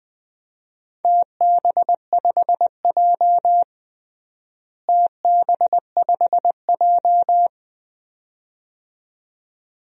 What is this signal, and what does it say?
Morse code, heard as a keyed tone, about 20 wpm, 700 Hz. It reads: TB5J TB5J
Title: DXCC-100 Call Signs - 4 Characters - Repeated 2x - 20wpm